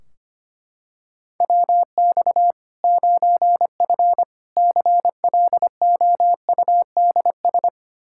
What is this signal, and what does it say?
Morse code, heard as a keyed tone, about 25 wpm, 700 Hz. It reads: WX 9F CLOUDS
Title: K1USN SST QSO Practice 25wpm